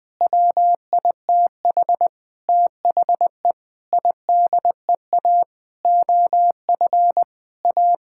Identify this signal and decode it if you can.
Morse code, heard as a keyed tone, about 20 wpm, 700 Hz. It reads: WITH THE IDEA OF A
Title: Sentences with 6-Word Phrases - Spoken Beginning - 20wpm